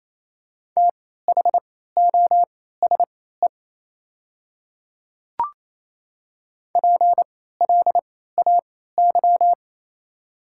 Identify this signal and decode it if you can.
Morse code, heard as a keyed tone, about 28 wpm, spaced as for 16 wpm, 700 Hz. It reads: THOSE PLAY
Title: 3x Character Spacing - Top 500 Words - 28wpm